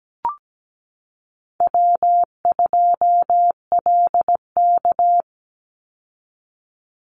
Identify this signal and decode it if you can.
Morse code, heard as a keyed tone, about 17 wpm, 700 Hz. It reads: W2LK
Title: Call Signs Repeated 2x - 17wpm